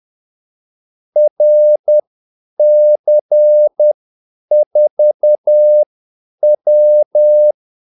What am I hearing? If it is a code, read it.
Morse code, heard as a keyed tone, about 10 wpm, spaced as for 8 wpm, 600 Hz. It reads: RC4W
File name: Callsigns-10-8wpm